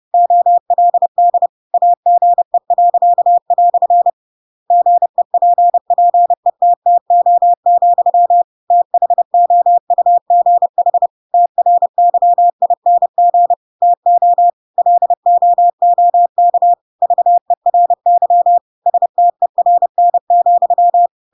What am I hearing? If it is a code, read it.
Morse code, heard as a keyed tone, about 30 wpm, 700 Hz. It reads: OLD AGE." GEPPETTO, THOUGH TRYING TO LOOK VERY STERN,
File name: Pinocchio / Pinocchio ch8 30 30 ik1hge